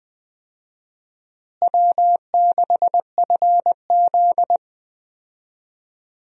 Morse code, heard as a keyed tone, about 20 wpm, 700 Hz. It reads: W6FZ
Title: USA Call Signs - 4 Characters - Repeated 2x - 20wpm